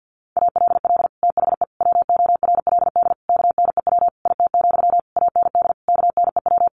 Morse code, heard as a keyed tone, about 50 wpm, 700 Hz. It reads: ALL THE WORLD KNEW IT, AND KNEW
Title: Sentences from Top 1200 Words 50wpm